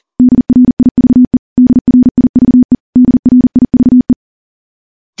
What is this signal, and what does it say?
Morse code, heard as a keyed tone, about 40 wpm, 250 Hz. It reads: DRIVE DRIVE DRIVE
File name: morse / morse-training-40WPM